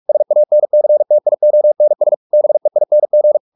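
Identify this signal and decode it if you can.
Morse code, heard as a keyed tone, about 44 wpm, 600 Hz. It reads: SANCTIONS BEING
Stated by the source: Morse practice at 45 wpm, the novel Never by Ken Follett in English, from CW med Gnister.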